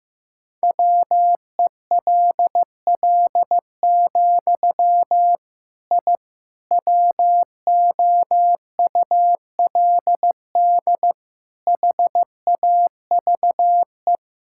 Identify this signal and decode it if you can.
Morse code, heard as a keyed tone, about 15 wpm, 700 Hz. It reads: WELL, I WOULD HAVE
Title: Sentences from Top 1000 Words - Review Part 5 - 15wpm